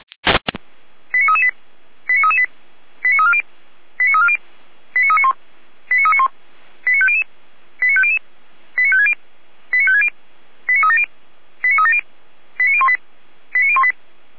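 Five-tone selective call (ZVEI), heard as quick runs of five tones, tones of about 70 ms. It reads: 89338 89338 89340 89340 89391 89391 89500 89500 89560 89560 89370 89370 89018 89018
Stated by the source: ZVEI five-tone call received by SDR, from audio.f85360.feuerwehr-orlamuende.org, a fire brigade (Feuerwehr Orlamünde) server.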